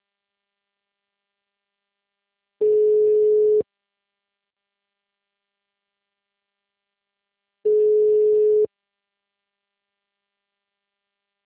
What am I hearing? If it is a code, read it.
Ringback tone (Europe, 425 Hz).